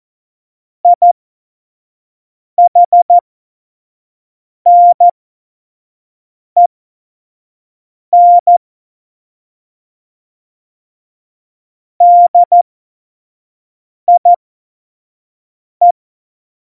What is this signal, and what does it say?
Morse code, heard as a keyed tone, about 14 wpm, spaced as for 5 wpm, 700 Hz. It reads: IHNEN DIE
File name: Morsekurs / Witze / Witz 22. Januar 2019, 25 F70